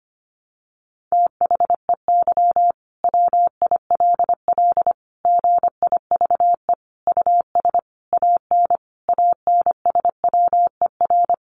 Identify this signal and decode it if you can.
Morse code, heard as a keyed tone, about 25 wpm, 700 Hz. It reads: THEY WILL GIVE US AN ANSWER
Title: Sentences from Top 600 Words 25wpm